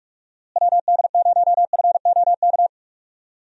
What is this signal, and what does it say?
Morse code, heard as a keyed tone, about 46 wpm, 700 Hz. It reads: WD0FOK
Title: USA Call Signs - 6 Characters - Repeated 3x - 45wpm